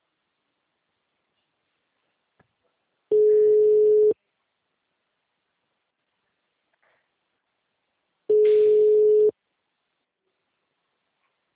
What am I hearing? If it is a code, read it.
Ringback tone (Europe, 425 Hz).
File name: out- -2002- - - .31857